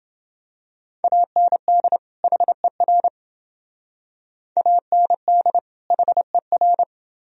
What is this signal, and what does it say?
Morse code, heard as a keyed tone, about 28 wpm, 700 Hz. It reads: AND HER AND HER